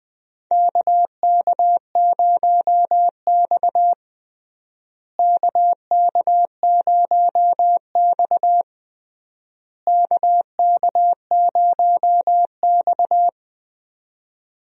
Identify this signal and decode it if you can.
Morse code, heard as a keyed tone, about 20 wpm, 700 Hz. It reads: KK0X KK0X KK0X
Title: USA Call Signs - 4 Characters - Repeated 3x - 20wpm